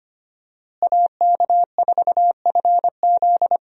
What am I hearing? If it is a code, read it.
Morse code, heard as a keyed tone, about 25 wpm, 700 Hz. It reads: AK4FZ
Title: USA Call Signs - 5 Characters - Repeated 2x - 25wpm